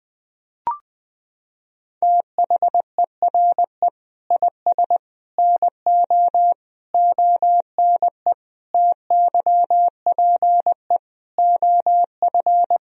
Morse code, heard as a keyed tone, about 21 wpm, 700 Hz. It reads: THERE IS NO ONE TYPE OF
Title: Sentences from Top 1000 Words - Review Part 3 - 20wpm